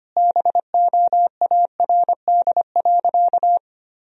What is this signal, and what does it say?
Morse code, heard as a keyed tone, about 25 wpm, 700 Hz. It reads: BOARD.
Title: Sentences from Top 1000 Words - Review Part 4 - 25wpm